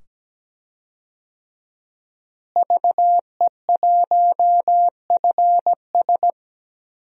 Morse code, heard as a keyed tone, about 17 wpm, 700 Hz. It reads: VE1FS
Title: DXCC-100 Call Signs - 5 Characters - Repeated 2x - 17wpm